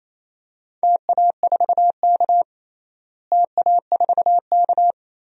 Morse code, heard as a keyed tone, about 28 wpm, 700 Hz. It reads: TA4K TA4K
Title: DXCC-100 Call Signs - 4 Characters - Repeated 2x - 28wpm